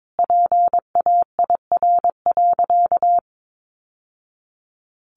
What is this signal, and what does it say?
Morse code, heard as a keyed tone, about 22 wpm, 700 Hz. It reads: PAIR.
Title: Sentences from Top 1200 Words 22wpm